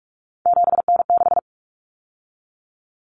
Morse code, heard as a keyed tone, about 44 wpm, 700 Hz. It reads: 7N6
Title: Common Cut Numbers - 3-Digit 45wpm